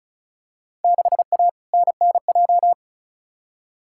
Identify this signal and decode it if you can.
Morse code, heard as a keyed tone, about 34 wpm, 700 Hz. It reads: 6A NNJ
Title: Field Day Exchanges Encore - 35wpm